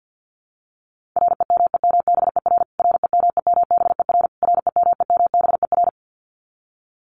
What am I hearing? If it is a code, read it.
Morse code, heard as a keyed tone, about 48 wpm, 700 Hz. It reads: REMEMBER REMEMBER REMEMBER
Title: Top 500 Words Repeated 50wpm